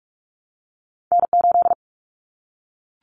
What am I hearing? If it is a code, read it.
Morse code, heard as a keyed tone, about 44 wpm, 700 Hz. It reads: N8
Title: Callsign Prefixes for Top 25 DXCC - 45wpm